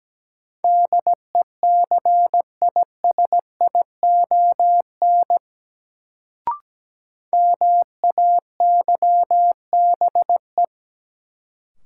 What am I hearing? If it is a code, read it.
Morse code, heard as a keyed tone, about 17 wpm, 700 Hz. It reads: DECISION MAYBE
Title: Sentences from Top 1000 Words - New Word Drill - 17wpm